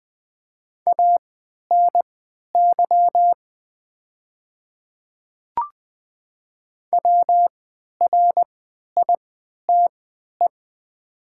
Morse code, heard as a keyed tone, about 20 wpm, spaced as for 11 wpm, 700 Hz. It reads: ANY WRITE
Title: 3x Character Spacing - Top 200 Words - 20wpm